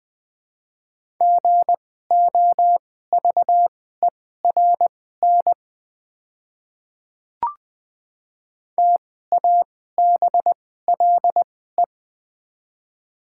Morse code, heard as a keyed tone, about 20 wpm, spaced as for 14 wpm, 700 Hz. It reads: GOVERN TABLE
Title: 2x Character Spacing - Top 500 Words - 20wpm